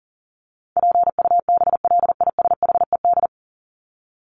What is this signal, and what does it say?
Morse code, heard as a keyed tone, about 40 wpm, 700 Hz. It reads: PUBLISHED
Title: ED Suffix - Top 100 Words with Base Words 40wpm